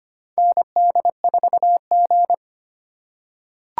Morse code, heard as a keyed tone, about 25 wpm, 700 Hz. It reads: ND4G
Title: K1USN SST QSO Practice - 25wpm